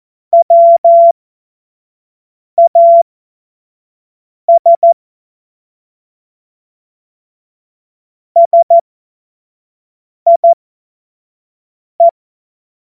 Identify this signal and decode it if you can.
Morse code, heard as a keyed tone, about 14 wpm, spaced as for 5 wpm, 680 Hz. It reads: WAS SIE
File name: 25 F70